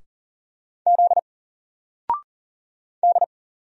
Morse code, heard as a keyed tone, about 40 wpm, 700 Hz. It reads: Z D
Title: Single Character (Letter:Number:Punctuation:Prosign) 40wpm